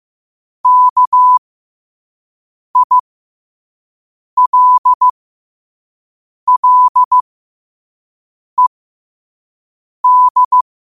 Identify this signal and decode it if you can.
Morse code, heard as a keyed tone, about 15 wpm, spaced as for 5 wpm, 1000 Hz. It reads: KILLED
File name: qotdfiles / 5wpm